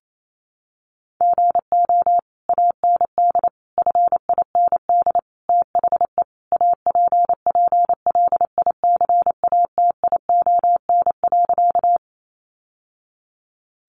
Morse code, heard as a keyed tone, about 28 wpm, 700 Hz. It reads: GO AND FIND THE APPLICATION.
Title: Sentences from Top 1200 Words 28wpm